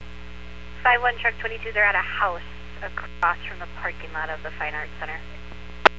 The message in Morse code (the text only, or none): none